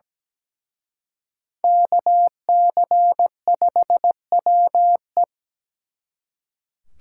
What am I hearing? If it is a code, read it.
Morse code, heard as a keyed tone, about 20 wpm, 700 Hz. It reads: KC5WE
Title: Speed Racing - USA 2x2 Call Signs 17wpm